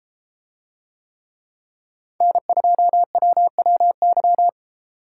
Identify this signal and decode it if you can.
Morse code, heard as a keyed tone, about 31 wpm, 700 Hz. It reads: N2WWY